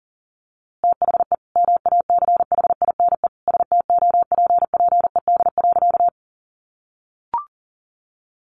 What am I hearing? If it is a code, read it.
Morse code, heard as a keyed tone, about 40 wpm, 700 Hz. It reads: THE MACHINE STOPPED.
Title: A Sound Of Thunder 40wpm